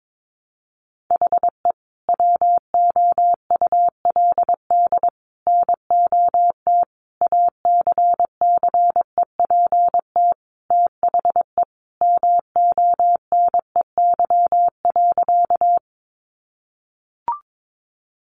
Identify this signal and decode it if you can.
Morse code, heard as a keyed tone, about 22 wpm, 700 Hz. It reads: HE WOULD NOT ACCEPT THE MONEY.
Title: Sentences from Top 1300 Words 22wpm